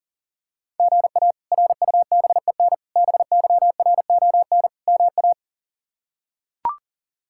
Morse code, heard as a keyed tone, about 40 wpm, 700 Hz. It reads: GA RUBEN BYRON MA
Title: K1USN SST QSO Practice - 40wpm